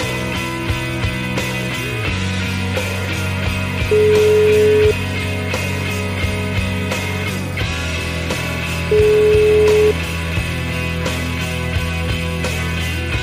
Ringback tone (Europe, 425 Hz), with noise.